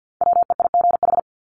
Morse code, heard as a keyed tone, about 48 wpm, 700 Hz. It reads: WEIGH
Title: Top 1751-2000 Words 50wpm